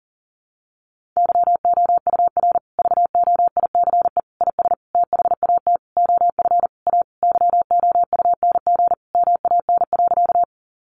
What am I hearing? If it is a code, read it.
Morse code, heard as a keyed tone, about 37 wpm, 700 Hz. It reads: YOUR VOICE IS THAT OF A YOUNG MAN.
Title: Sentences from Top 500 Words 40wpm